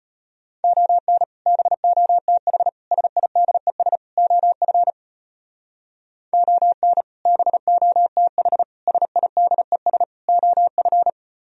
Morse code, heard as a keyed tone, about 32 wpm, 700 Hz. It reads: ON BOTH SIDES OF ON BOTH SIDES OF